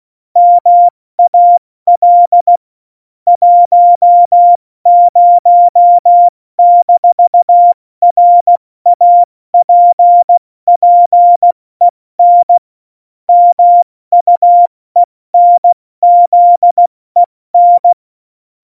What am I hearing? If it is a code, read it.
Morse code, heard as a keyed tone, about 16 wpm, 700 Hz. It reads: MAL 10-RAPPEN MUENZEN
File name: Morsekurs / Witze / Witz 15. Oktober 2019, 70 F80